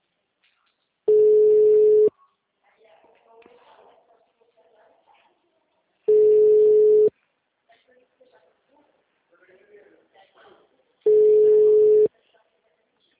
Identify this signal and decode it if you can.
Ringback tone (Europe, 425 Hz), with speech.